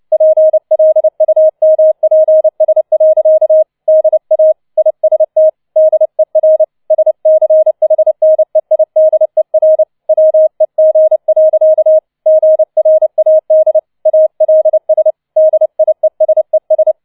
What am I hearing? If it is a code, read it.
Morse code, heard as a keyed tone, about 29 wpm, 630 Hz. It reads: PLUMPS. DA IST DER SCHNEIDER WEG. GRAD ALS DIESES